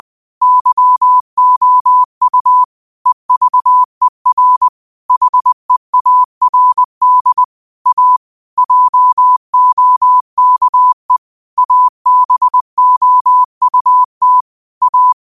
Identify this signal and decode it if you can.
Morse code, heard as a keyed tone, about 20 wpm, 1000 Hz. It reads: YOU EVER HEARD A JOKE ABOUT A
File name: qotdfiles / 20wpm